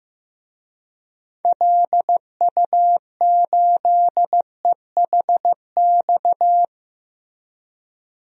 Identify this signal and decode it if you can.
Morse code, heard as a keyed tone, about 15 wpm, 700 Hz. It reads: LU8EHX
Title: DXCC-100 Call Signs - 6 Characters - Repeated 3x - 15wpm